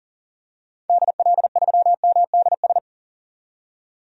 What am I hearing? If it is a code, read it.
Morse code, heard as a keyed tone, about 40 wpm, 700 Hz. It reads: DL3MDS
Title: DXCC-100 Call Signs - 6 Characters - 40wpm